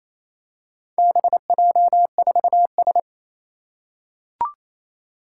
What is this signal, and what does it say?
Morse code, heard as a keyed tone, about 29 wpm, 700 Hz. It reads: BJ4S
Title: Four Challenging Letter-Number Combinations 28wpm